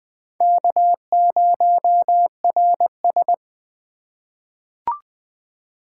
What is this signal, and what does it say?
Morse code, heard as a keyed tone, about 20 wpm, 700 Hz. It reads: K0RS
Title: 2023 SOTA Chaser Call Signs - 20wpm